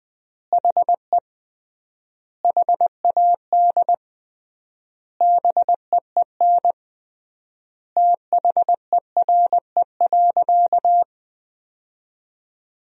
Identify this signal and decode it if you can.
Morse code, heard as a keyed tone, about 21 wpm, 700 Hz. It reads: HE HAD BEEN THERE.
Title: 3x Word-Spacing - Sentences from Top 300 Words - 20wpm